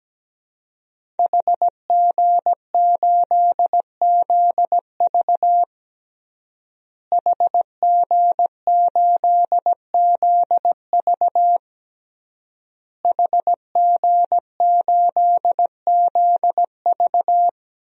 Morse code, heard as a keyed tone, about 17 wpm, 700 Hz. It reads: HG8ZV HG8ZV HG8ZV
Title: DXCC-100 Call Signs - 5 Characters - Repeated 3x - 17wpm